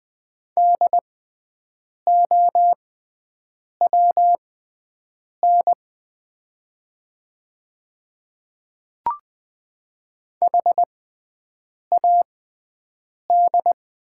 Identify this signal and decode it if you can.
Morse code, heard as a keyed tone, about 20 wpm, spaced as for 7 wpm, 700 Hz. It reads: DOWN HAD
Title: 6x Character Spacing - Top 100 Words - 20wpm